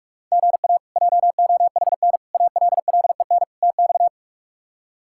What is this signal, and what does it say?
Morse code, heard as a keyed tone, about 46 wpm, 700 Hz. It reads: GA JOHN ALLEN TX
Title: K1USN SST QSO Practice - 45wpm